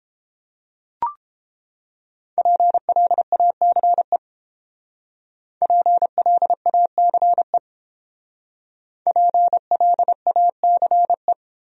Morse code, heard as a keyed tone, about 29 wpm, 700 Hz. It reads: PLACE PLACE PLACE